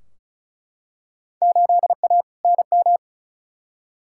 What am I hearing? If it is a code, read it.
Morse code, heard as a keyed tone, about 36 wpm, 700 Hz. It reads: 8A NM
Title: Field Day Exchanges Encore - 35wpm